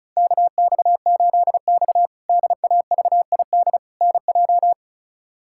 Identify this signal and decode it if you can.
Morse code, heard as a keyed tone, about 36 wpm, 700 Hz. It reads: KX8X DAVID NJ
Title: K1USN SST QSO Practice - 35wpm